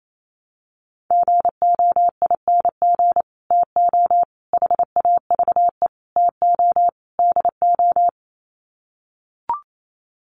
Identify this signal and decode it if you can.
Morse code, heard as a keyed tone, about 28 wpm, 700 Hz. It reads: GOING TO HAVE TO DO
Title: Sets of 5 Words Encore 28wpm